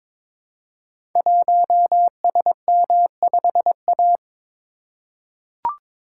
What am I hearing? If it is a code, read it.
Morse code, heard as a keyed tone, about 22 wpm, 700 Hz. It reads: 1SM5A
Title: Code Groups - 5 Letter:Numbers - 22wpm